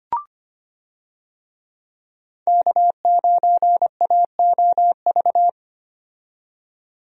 Morse code, heard as a keyed tone, about 25 wpm, 700 Hz. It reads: K9AOV
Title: USA Call Signs - 5 Characters - 25wpm